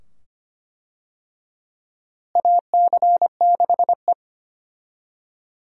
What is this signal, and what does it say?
Morse code, heard as a keyed tone, about 25 wpm, 700 Hz. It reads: AC6E
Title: USA Call Signs - 4 Characters - Repeated 2x - 25wpm